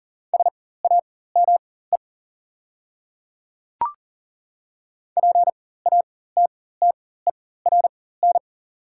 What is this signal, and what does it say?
Morse code, heard as a keyed tone, about 40 wpm, spaced as for 19 wpm, 690 Hz. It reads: SAME PATTERN
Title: 4x Character Spacing - Top 500 Words - 40wpm